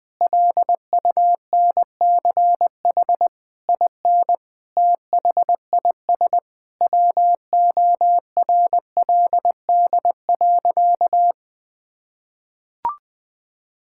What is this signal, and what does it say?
Morse code, heard as a keyed tone, about 20 wpm, 700 Hz. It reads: LUNCH IN THIS WORLD.
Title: Sentences with 9-Word Phrases - 20wpm